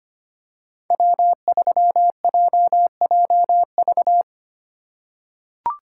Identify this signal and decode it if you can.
Morse code, heard as a keyed tone, about 25 wpm, 700 Hz. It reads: W3JJV
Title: USA Call Signs - 5 Characters - 8x Character Spacing - 25wpm